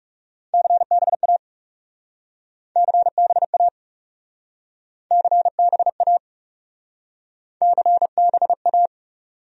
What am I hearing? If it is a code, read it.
Morse code, heard as a keyed tone, about 37 wpm, 700 Hz. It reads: CBA CBA CBA CBA
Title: Speed-Racing Abbreviations - 30wpm